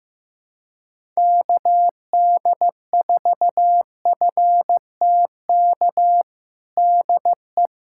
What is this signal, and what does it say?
Morse code, heard as a keyed tone, about 15 wpm, 700 Hz. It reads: KD4FTK DE
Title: USA Callsign DE Callsign 15wpm